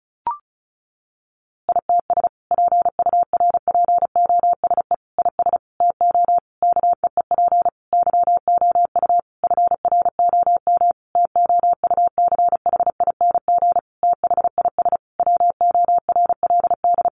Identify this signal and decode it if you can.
Morse code, heard as a keyed tone, about 34 wpm, 700 Hz. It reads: ITS PURPOSE IS TO KEEP YOU FROM TOUCHING THIS WORLD
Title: A Sound Of Thunder 35wpm